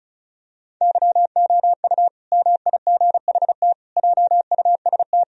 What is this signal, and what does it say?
Morse code, heard as a keyed tone, about 34 wpm, 700 Hz. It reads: YOU MIGHT JUST
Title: Sentences from Top 1000 Words - Review Part 2 - 35wpm